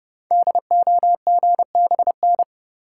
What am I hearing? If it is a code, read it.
Morse code, heard as a keyed tone, about 30 wpm, 700 Hz. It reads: DOGBN
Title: Code Groups - 5 Letters - 30wpm